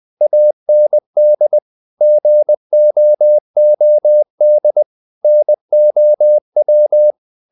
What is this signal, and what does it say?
Morse code, heard as a keyed tone, about 20 wpm, 600 Hz. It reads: AND GOOD NOW
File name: the old man and the sea part 2 20wpm